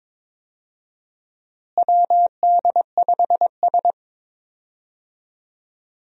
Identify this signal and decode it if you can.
Morse code, heard as a keyed tone, about 21 wpm, 700 Hz. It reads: WD5S
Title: USA Call Signs - 4 Characters - Repeated 2x - 22wpm